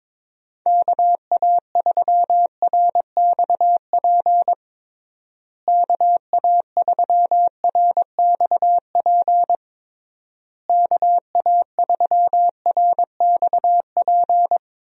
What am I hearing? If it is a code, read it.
Morse code, heard as a keyed tone, about 22 wpm, 700 Hz. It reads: KA3RXP KA3RXP KA3RXP
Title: USA Call Signs - 6 Characters - Repeated 3x - 22wpm